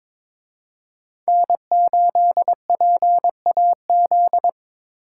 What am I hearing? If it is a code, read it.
Morse code, heard as a keyed tone, about 26 wpm, 700 Hz. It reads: N8PAZ
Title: Speed Racing - USA 1x3 Call Signs 22wpm